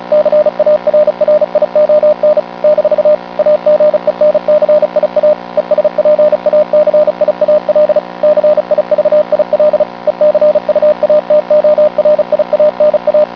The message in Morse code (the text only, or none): CARRION - AGENCIA ESPACIAL CIVIL ECUATORIANA